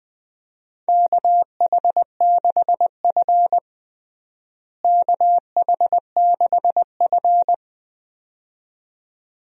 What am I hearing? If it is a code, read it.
Morse code, heard as a keyed tone, about 20 wpm, 700 Hz. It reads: KH6F KH6F
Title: DXCC-100 Call Signs - 4 Characters - Repeated 2x - 20wpm